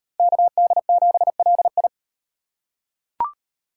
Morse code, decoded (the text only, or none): KD7LI